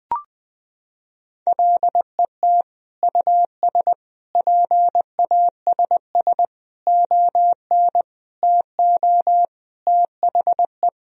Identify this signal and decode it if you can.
Morse code, heard as a keyed tone, about 20 wpm, 700 Hz. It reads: LET US PASS ON TO THE
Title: Sentences from Top 1000 Words - Review Part 1 - 20wpm